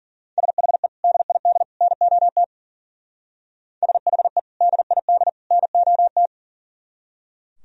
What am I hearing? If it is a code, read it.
Morse code, heard as a keyed tone, about 49 wpm, 700 Hz. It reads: SHE DID NOT SHE DID NOT
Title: Speed-Racing - Sets of 3 Words 40wpm